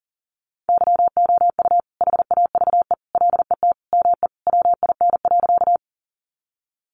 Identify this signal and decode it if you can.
Morse code, heard as a keyed tone, about 40 wpm, 700 Hz. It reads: YOU HAVE LET ME WIN.
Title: Sentences from Top 1500 Words 40wpm